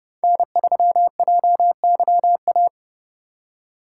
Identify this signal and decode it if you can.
Morse code, heard as a keyed tone, about 30 wpm, 700 Hz. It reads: N3JYA
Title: USA Call Signs - 5 Characters - 3x Character Spacing - 30wpm